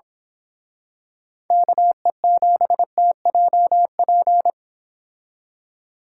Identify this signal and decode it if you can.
Morse code, heard as a keyed tone, about 26 wpm, 700 Hz. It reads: KE7TJP
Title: Speed Racing - USA Call Signs 22wpm